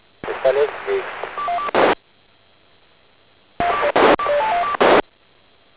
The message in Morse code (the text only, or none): none